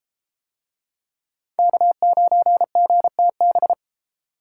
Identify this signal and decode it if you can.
Morse code, heard as a keyed tone, about 30 wpm, 700 Hz. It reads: K9GTB